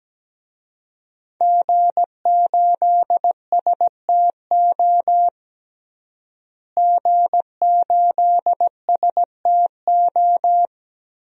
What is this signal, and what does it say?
Morse code, heard as a keyed tone, about 17 wpm, 700 Hz. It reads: G8STO G8STO
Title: DXCC-100 Call Signs - 5 Characters - Repeated 3x - 17wpm